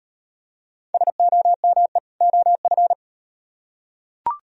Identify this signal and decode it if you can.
Morse code, heard as a keyed tone, about 38 wpm, 700 Hz. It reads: SOME OF